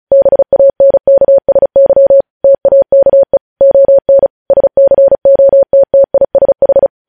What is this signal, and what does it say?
Morse code, heard as a keyed tone, about 36 wpm, 550 Hz. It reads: BANKSY TAKE ON SCOTTISH
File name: podcast / News headlines-1x-35x35